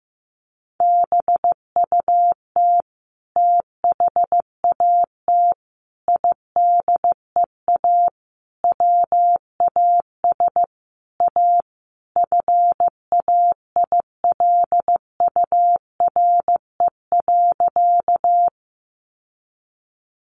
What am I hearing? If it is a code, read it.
Morse code, heard as a keyed tone, about 15 wpm, 700 Hz. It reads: BUT THAT IDEA WAS A FAILURE.
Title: Sentences from Top 900 Words 15wpm